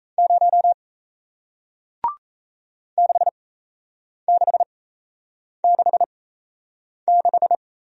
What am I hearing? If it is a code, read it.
Morse code, heard as a keyed tone, about 38 wpm, spaced as for 8 wpm, 700 Hz. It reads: 0 6666